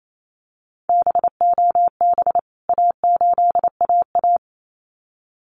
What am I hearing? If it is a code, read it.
Morse code, heard as a keyed tone, about 28 wpm, 700 Hz. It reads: BOB A8AA
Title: CWT Exchange 28wpm